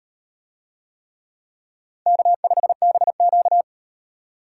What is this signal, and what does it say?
Morse code, heard as a keyed tone, about 38 wpm, 700 Hz. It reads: K5BQ